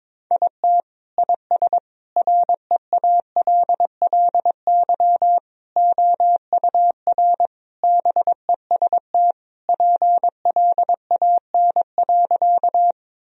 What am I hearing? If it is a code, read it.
Morse code, heard as a keyed tone, about 22 wpm, 700 Hz. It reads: IT IS REALLY OUR BEST PLAN.
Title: Sentences from Top 1000 Words - Review Part 2 - 22wpm